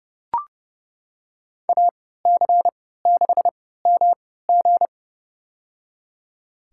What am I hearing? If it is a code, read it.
Morse code, heard as a keyed tone, about 30 wpm, spaced as for 17 wpm, 700 Hz. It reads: AC6MG
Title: USA Call Signs - 5 Characters - 3x Character Spacing - 30wpm